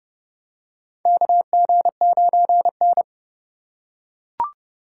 Morse code, heard as a keyed tone, about 32 wpm, 700 Hz. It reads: KG9N
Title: USA Call Signs - 4 Characters - 3x Character Spacing - 30wpm